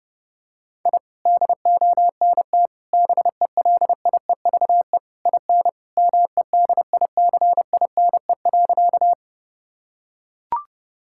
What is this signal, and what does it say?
Morse code, heard as a keyed tone, about 30 wpm, 700 Hz. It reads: I DONT BELIEVE IN MEDICINE.
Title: Sentences from Top 1000 Words - Review Part 6 - 30wpm